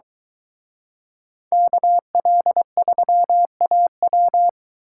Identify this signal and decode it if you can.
Morse code, heard as a keyed tone, about 24 wpm, 700 Hz. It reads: KL3AW